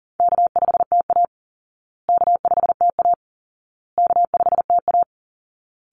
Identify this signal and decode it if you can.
Morse code, heard as a keyed tone, about 40 wpm, 700 Hz. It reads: K5TA K5TA K5TA
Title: Call Signs Repeated 2x - 40wpm